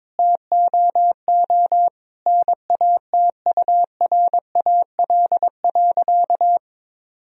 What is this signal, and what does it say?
Morse code, heard as a keyed tone, about 22 wpm, 700 Hz. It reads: TOO NATURAL.
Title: Sentences from Top 1000 Words - Review Part 4 - 22wpm